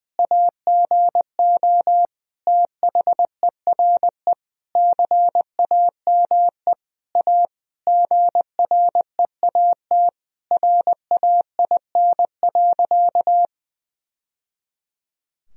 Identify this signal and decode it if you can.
Morse code, heard as a keyed tone, about 20 wpm, 700 Hz. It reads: AGO THERE CAME A GREAT RAIN.
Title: Sentences from Top 1000 Words - Review Part 3 - 20wpm